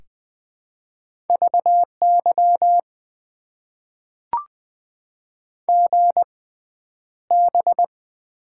Morse code, heard as a keyed tone, about 20 wpm, spaced as for 10 wpm, 700 Hz. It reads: VY GB
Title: Top 90 QSO Elements - 6x Character Spacing - 20wpm